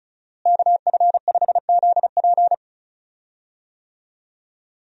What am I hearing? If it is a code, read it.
Morse code, heard as a keyed tone, about 34 wpm, 700 Hz. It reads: KF5ZP
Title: USA Call Signs - 5 Characters - Repeated 3x - 35wpm